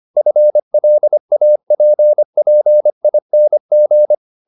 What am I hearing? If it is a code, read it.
Morse code, heard as a keyed tone, about 25 wpm, 600 Hz. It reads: FLAPPING